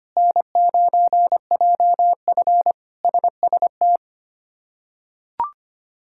Morse code, heard as a keyed tone, about 25 wpm, 700 Hz. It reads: N9JF SST
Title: K1USN SST QSO Practice - 25wpm